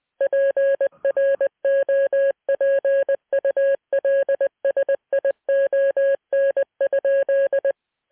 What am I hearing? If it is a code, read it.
Morse code, heard as a keyed tone, about 19 wpm, 550 Hz. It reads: PROPULSION?